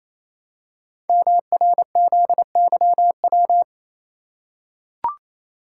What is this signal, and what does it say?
Morse code, heard as a keyed tone, about 28 wpm, 700 Hz. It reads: MRZYW
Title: Code Groups - 5 Letters - 28wpm